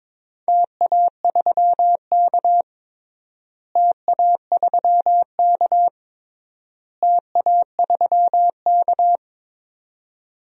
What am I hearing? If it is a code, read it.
Morse code, heard as a keyed tone, about 22 wpm, 700 Hz. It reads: TA3K TA3K TA3K
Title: DXCC-100 Call Signs - 4 Characters - Repeated 3x - 22wpm